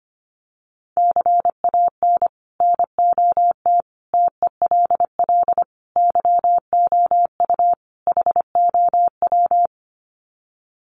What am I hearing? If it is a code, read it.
Morse code, heard as a keyed tone, about 25 wpm, 700 Hz. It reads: CAN NOT TELL YOU HOW
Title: Sets of 5 Words Encore2 25wpm